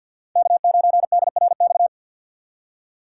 Morse code, heard as a keyed tone, about 53 wpm, 700 Hz. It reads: K9DRX
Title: USA Call Signs - 3x Character Spacing - 50wpm